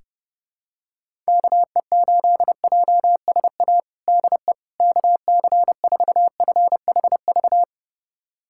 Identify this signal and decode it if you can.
Morse code, heard as a keyed tone, about 30 wpm, 700 Hz. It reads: KE8JSA DE KC4FHV
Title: USA Callsign DE Callsign - 6 Characters - 30wpm